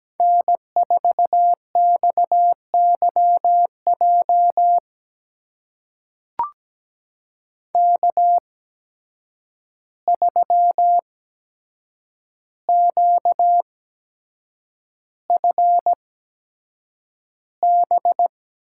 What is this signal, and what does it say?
Morse code, heard as a keyed tone, about 17 wpm, spaced as for 7 wpm, 700 Hz. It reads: N4XYJ K3QFB
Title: USA Call Signs - 5 Characters - 8x Character Spacing - 17wpm